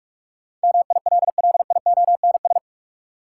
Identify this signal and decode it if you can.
Morse code, heard as a keyed tone, about 47 wpm, 700 Hz. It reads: MILLIONS
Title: Sets of 5 Words - New Word Drill - 45wpm